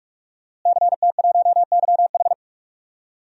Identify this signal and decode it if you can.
Morse code, heard as a keyed tone, about 46 wpm, 700 Hz. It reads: CT1YH
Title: DXCC-100 Call Signs - 5 Characters - Repeated 2x - 45wpm